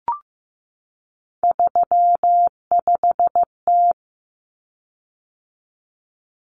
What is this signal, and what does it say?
Morse code, heard as a keyed tone, about 15 wpm, 700 Hz. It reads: 35T
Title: Common Cut Numbers - 3-Digit 15wpm